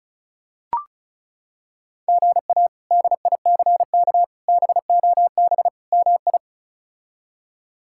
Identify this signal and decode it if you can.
Morse code, heard as a keyed tone, about 34 wpm, 700 Hz. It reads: GA DICK BOB MI
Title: K1USN SST QSO Practice - 35wpm